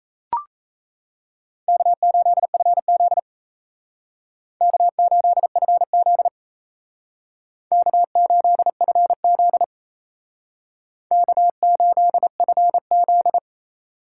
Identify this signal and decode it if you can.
Morse code, heard as a keyed tone, about 34 wpm, 700 Hz. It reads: K8FZ K8FZ K8FZ K8FZ
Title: Speed Racing - USA 1x2 Call Signs 28wpm